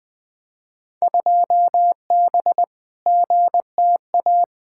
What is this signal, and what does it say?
Morse code, heard as a keyed tone, about 20 wpm, 700 Hz. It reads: 2B GTA
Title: Field Day Exchanges Encore - 20wpm